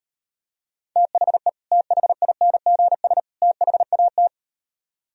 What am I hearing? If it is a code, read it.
Morse code, heard as a keyed tone, about 38 wpm, 700 Hz. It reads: THE THINGS THAT